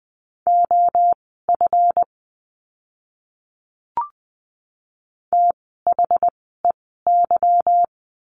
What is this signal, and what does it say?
Morse code, heard as a keyed tone, about 20 wpm, spaced as for 15 wpm, 700 Hz. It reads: OF THEY
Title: 185 Lesson 30f - Letter b - Words 20wpm